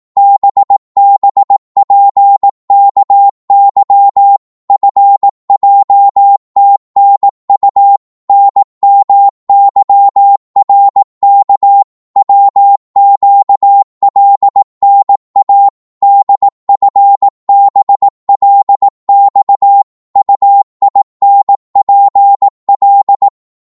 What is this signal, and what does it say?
Morse code, heard as a keyed tone, about 18 wpm, 800 Hz. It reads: BBPKY FJTNU NMYRK WQLNA DFBLX UINPL
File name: Morse / ltr-18wpm